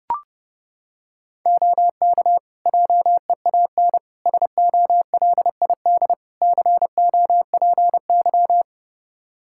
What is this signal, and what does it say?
Morse code, heard as a keyed tone, about 32 wpm, 700 Hz. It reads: OK JEAN SOLID COPY
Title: Practice QSO - Exchange 3 - More Info - 30wpm